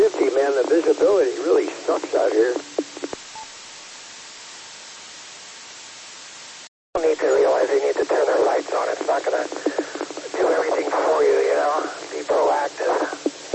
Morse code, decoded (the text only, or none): none